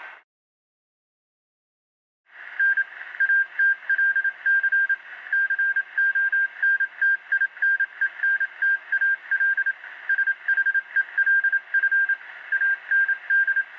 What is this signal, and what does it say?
Morse code, heard as a keyed tone, about 27 wpm, 1600 Hz. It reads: N ATLC CONTINENTAL SHELF AND